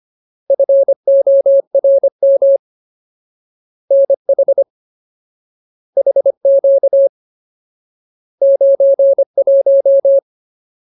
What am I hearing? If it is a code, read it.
Morse code, heard as a keyed tone, about 25 wpm, 550 Hz. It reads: FORM NH HQ 91
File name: Ultimate Word List / 25WPM Ultimate